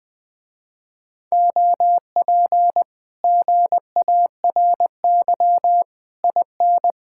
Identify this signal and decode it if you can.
Morse code, heard as a keyed tone, about 21 wpm, 700 Hz. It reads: OP GARY IN
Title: Speed-Racing Common QSO Phrases - 20wpm